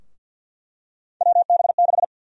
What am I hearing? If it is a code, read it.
Morse code, heard as a keyed tone, about 47 wpm, 700 Hz. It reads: WB6
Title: USA Call Sign Prefixes - 50wpm